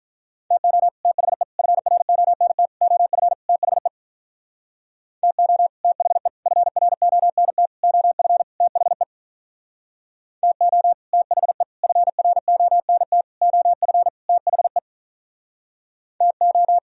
Morse code, heard as a keyed tone, about 43 wpm, 700 Hz. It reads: TO THE FRONT OF THE TO THE FRONT OF THE TO THE FRONT OF THE TO
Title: Speed-Racing - Sets of 5 Words 35wpm